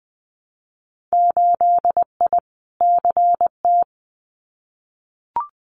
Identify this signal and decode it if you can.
Morse code, heard as a keyed tone, about 20 wpm, 700 Hz. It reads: 8I CT
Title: Winter Field Day Exchanges Encore 20wpm